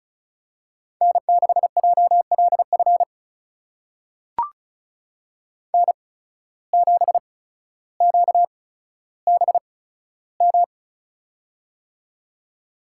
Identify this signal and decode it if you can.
Morse code, heard as a keyed tone, about 36 wpm, spaced as for 15 wpm, 700 Hz. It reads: N6JLF N7QBM
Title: USA Call Signs - 5 Characters - 8x Character Spacing - 35wpm